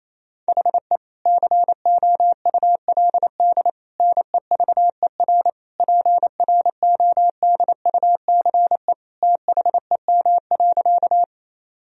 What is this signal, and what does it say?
Morse code, heard as a keyed tone, about 27 wpm, 700 Hz. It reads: HE COULD NEVER PRODUCE THEM.
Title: Sentences from Top 1000 Words - Review Part 3 - 28wpm